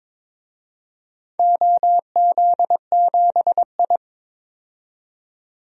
Morse code, heard as a keyed tone, about 22 wpm, 700 Hz. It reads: OZ7I
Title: DXCC-100 Call Signs - 4 Characters - 22wpm